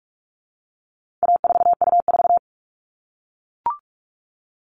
Morse code, heard as a keyed tone, about 44 wpm, 700 Hz. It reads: A4U4
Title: All Cut Numbers - 4-Digit 45wpm